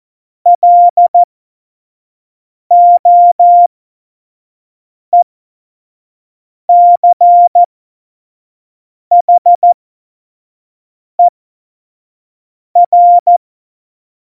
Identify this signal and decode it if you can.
Morse code, heard as a keyed tone, about 14 wpm, spaced as for 5 wpm, 700 Hz. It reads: LOECHER
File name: Morsekurs / Witze / Witz 25. Dezember 2018, 25 F70